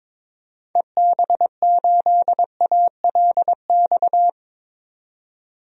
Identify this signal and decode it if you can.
Morse code, heard as a keyed tone, about 21 wpm, 700 Hz. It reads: EB8ALX
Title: DXCC-100 Call Signs - 6 Characters - 22wpm